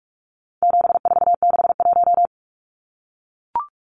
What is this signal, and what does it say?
Morse code, heard as a keyed tone, about 44 wpm, 700 Hz. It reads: 7461
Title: Numbers - 4-Digit 45wpm